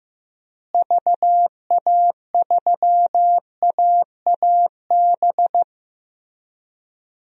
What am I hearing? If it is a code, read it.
Morse code, heard as a keyed tone, about 15 wpm, 700 Hz. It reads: VA3AAB
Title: DXCC-100 Call Signs - 6 Characters - Repeated 3x - 15wpm